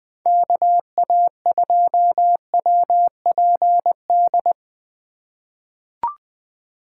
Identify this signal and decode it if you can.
Morse code, heard as a keyed tone, about 20 wpm, 700 Hz. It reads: KA2WPD
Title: USA Call Signs - 6 Characters - 4x Character Spacing - 20wpm